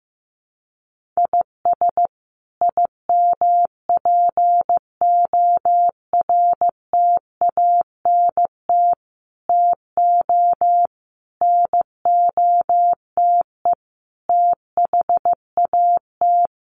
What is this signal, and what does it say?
Morse code, heard as a keyed tone, about 15 wpm, 700 Hz. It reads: IS IMPORTANT TO NOTE THAT
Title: Sets of 5 Words 15wpm